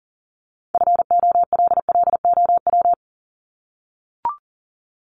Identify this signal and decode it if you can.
Morse code, heard as a keyed tone, about 40 wpm, 700 Hz. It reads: FOLLOW
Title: ING Suffix - Top 100 Words with Base Words 40wpm